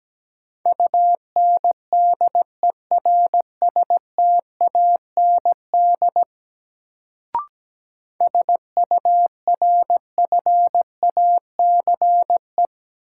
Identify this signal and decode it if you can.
Morse code, heard as a keyed tone, about 17 wpm, 700 Hz. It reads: UNDERSTAND SURFACE
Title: Sentences from Top 600 Words - New Word Drill - 17wpm